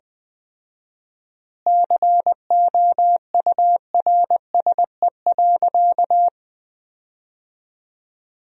Morse code, heard as a keyed tone, about 21 wpm, 700 Hz. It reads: COURSE.
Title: 4x Word-Spacing - Sentences from Top 500 Words - 20wpm